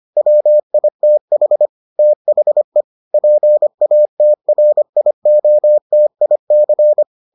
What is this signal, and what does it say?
Morse code, heard as a keyed tone, about 25 wpm, 600 Hz. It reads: WITH THE PATRIOTIC